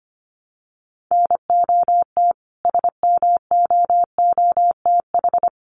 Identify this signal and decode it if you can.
Morse code, heard as a keyed tone, about 25 wpm, 700 Hz. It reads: NOT SMOOTH
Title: Sentences from Top 1400 Words 25wpm